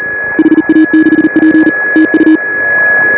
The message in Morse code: HA6P TU

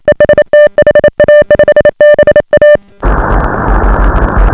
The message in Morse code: EST HA5BA

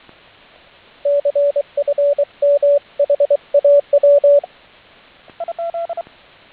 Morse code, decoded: CFM HAW

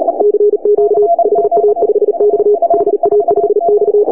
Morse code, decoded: CQ HA5X HA5X

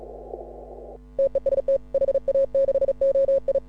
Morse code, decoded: TEST HA6OI

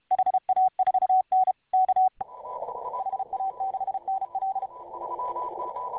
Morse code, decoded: HA4N K HA6NL

